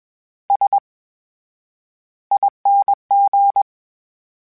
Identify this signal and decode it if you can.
Morse code, heard as a keyed tone, about 21 wpm, spaced as for 10 wpm, 800 Hz. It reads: SING
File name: Special INGs-mp3s / Special INGs-mp3s / VerySlowToFaster Ing @20